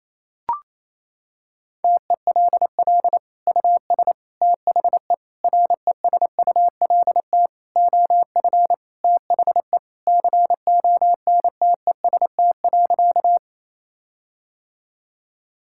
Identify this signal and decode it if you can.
Morse code, heard as a keyed tone, about 28 wpm, 700 Hz. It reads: TELL US THE RESULT OF THE CONTEST.
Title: Sentences from Top 1600 Words 28wpm